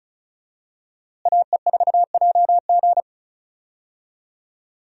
Morse code, heard as a keyed tone, about 36 wpm, 700 Hz. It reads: AE4JG
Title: USA Call Signs - 5 Characters - Repeated 3x - 35wpm